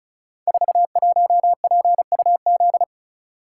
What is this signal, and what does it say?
Morse code, heard as a keyed tone, about 36 wpm, 700 Hz. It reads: 41PUZ
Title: Code Groups - 5 Letter:Numbers - 35wpm